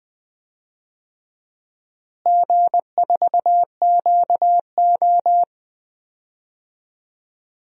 Morse code, heard as a keyed tone, about 21 wpm, 690 Hz. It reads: G4QO
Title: DXCC-100 Call Signs - 4 Characters - Repeated 3x - 20wpm